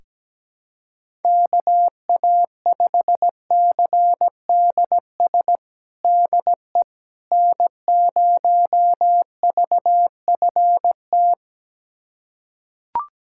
Morse code, heard as a keyed tone, about 17 wpm, 700 Hz. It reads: KA5CDS DE N0VFT
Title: USA Callsign DE Callsign 17wpm